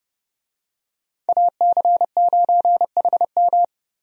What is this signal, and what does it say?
Morse code, heard as a keyed tone, about 29 wpm, 700 Hz. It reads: AC9HM
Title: USA Call Signs - 5 Characters - 30wpm